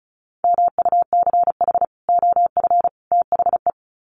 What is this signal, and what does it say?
Morse code, heard as a keyed tone, about 34 wpm, 700 Hz. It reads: MUCH OF THE
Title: Sets of 3 Words 35wpm